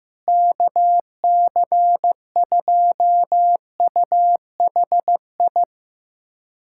Morse code, heard as a keyed tone, about 15 wpm, 700 Hz. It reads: KC2UHI
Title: USA Callsign DE Callsign 15wpm